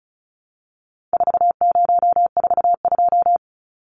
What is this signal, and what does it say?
Morse code, heard as a keyed tone, about 34 wpm, 700 Hz. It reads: 4042
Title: Numbers - 4-Digit 35wpm